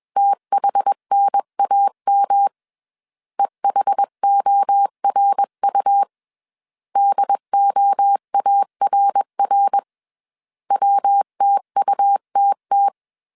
Morse code, heard as a keyed tone, about 21 wpm, 800 Hz. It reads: THNAM EHORU DOARR WTUTT